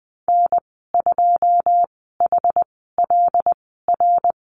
Morse code, heard as a keyed tone, about 20 wpm, spaced as for 15 wpm, 700 Hz. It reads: N2HLR
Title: 122 Lesson 20g - Number 3 - Call Signs 20wpm